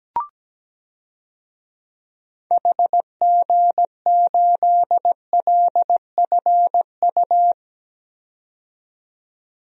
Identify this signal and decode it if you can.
Morse code, heard as a keyed tone, about 17 wpm, 700 Hz. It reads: HG8LFU
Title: DXCC-100 Call Signs - 6 Characters - 17wpm